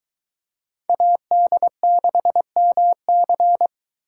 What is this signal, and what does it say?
Morse code, heard as a keyed tone, about 23 wpm, 700 Hz. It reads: AD6MC